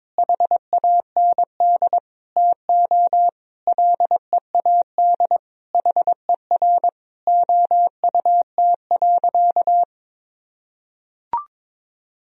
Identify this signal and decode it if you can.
Morse code, heard as a keyed tone, about 21 wpm, 700 Hz. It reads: HAND TO LEAD HER OUT.
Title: Sentences from Top 1000 Words - Review Part 2 - 22wpm